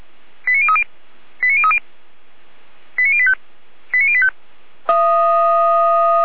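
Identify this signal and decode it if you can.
Five-tone selective call (ZVEI), heard as quick runs of five tones, tones of about 70 ms. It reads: 89030 89030 89085 89085
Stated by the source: ZVEI five-tone call 89080 received by SDR.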